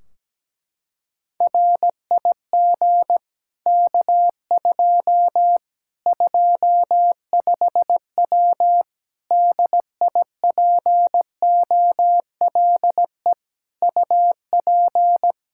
Morse code, heard as a keyed tone, about 17 wpm, 700 Hz. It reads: RIG K2 25W DIPOLE UP
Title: Practice QSO - Exchange 2 - Info - 17wpm